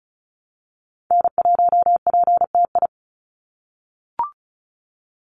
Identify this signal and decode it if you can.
Morse code, heard as a keyed tone, about 34 wpm, 700 Hz. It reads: N1PTI
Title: USA Call Signs 35wpm